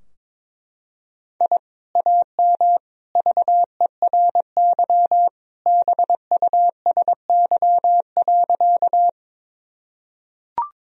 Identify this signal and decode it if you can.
Morse code, heard as a keyed tone, about 22 wpm, 700 Hz. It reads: I AM VERY BUSY.
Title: Sentences from Top 1000 Words - Review Part 3 - 22wpm